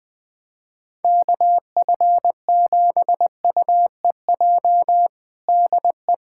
Morse code, heard as a keyed tone, about 20 wpm, 700 Hz. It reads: KF7UEJ DE
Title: Practice QSO - Exchange 4 - Ending - 20wpm